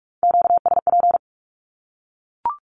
Morse code, heard as a keyed tone, about 44 wpm, 700 Hz. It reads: QSP
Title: Q-Codes 45wpm